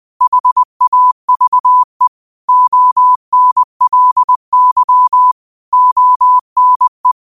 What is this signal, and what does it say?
Morse code, heard as a keyed tone, about 20 wpm, 1000 Hz. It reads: HAVE ONLY ONE
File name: qotdfiles / 20wpm